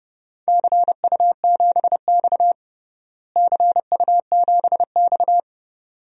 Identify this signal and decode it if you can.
Morse code, heard as a keyed tone, about 30 wpm, 700 Hz. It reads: CU7X CU7X
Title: DXCC-100 Call Signs - 4 Characters - Repeated 2x - 30wpm